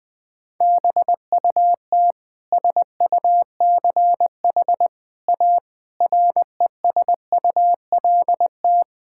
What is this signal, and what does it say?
Morse code, heard as a keyed tone, about 20 wpm, 700 Hz. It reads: BUT SUCH A RESULT
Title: Sentences from Top 1000 Words - Review Part 5 - 20wpm